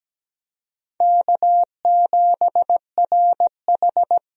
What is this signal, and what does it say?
Morse code, heard as a keyed tone, about 17 wpm, 700 Hz. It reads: K7RH
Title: USA Call Signs - 4 Characters - Repeated 3x - 17wpm